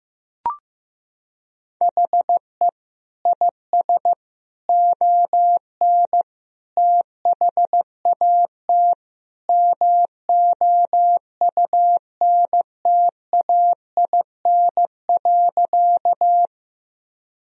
Morse code, heard as a keyed tone, about 15 wpm, 700 Hz. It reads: HE IS ON THAT MOUNTAIN.
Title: Sentences from Top 1000 Words - Review Part 1 - 15wpm